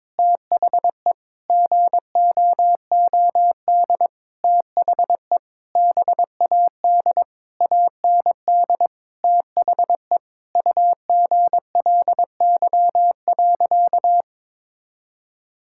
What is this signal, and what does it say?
Morse code, heard as a keyed tone, about 22 wpm, 700 Hz. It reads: THE GOOD THE BAD AND THE UGLY.
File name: Sentences with 7-Word Phrases - 22wpm